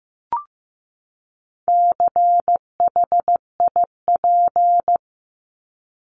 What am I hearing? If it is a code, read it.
Morse code, heard as a keyed tone, about 15 wpm, 700 Hz. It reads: CHIP
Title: Top 1751-2000 Words 15wpm